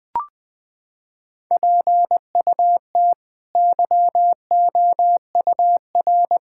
Morse code, heard as a keyed tone, about 20 wpm, 700 Hz. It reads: PUT YOUR
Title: Sets of 8 Words - 20wpm